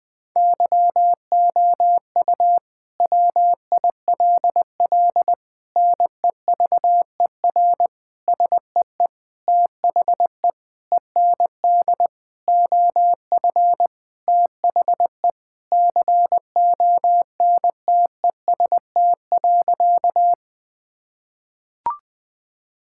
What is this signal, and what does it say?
Morse code, heard as a keyed tone, about 20 wpm, 700 Hz. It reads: YOU WILL NEVER SEE THE END OF THE CONTEST.
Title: Sentences from Top 1600 Words 20wpm